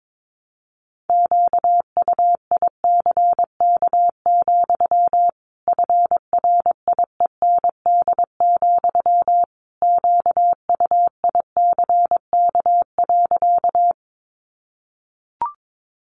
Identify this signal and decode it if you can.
Morse code, heard as a keyed tone, about 22 wpm, 700 Hz. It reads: QUICK, FRIEND, QUICK.
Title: Sentences from Top 500 Words 22wpm